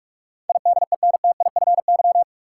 Morse code, heard as a keyed tone, about 46 wpm, 700 Hz. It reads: IDENTIFY
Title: Sentences from Top 900 Words - New Word Drill - 45wpm